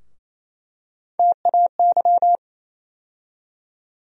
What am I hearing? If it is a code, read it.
Morse code, heard as a keyed tone, about 28 wpm, 700 Hz. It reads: TAY
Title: USA Call Sign Postfixes - 3 Letters 28wpm